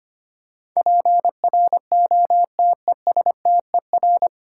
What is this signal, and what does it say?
Morse code, heard as a keyed tone, about 25 wpm, 700 Hz. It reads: PROTESTER
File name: Nine Letter Words - 25wpm